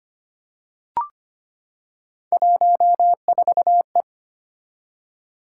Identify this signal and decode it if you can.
Morse code, heard as a keyed tone, about 25 wpm, 700 Hz. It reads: 14E
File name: Three Letter-Number Combinations - 25wpm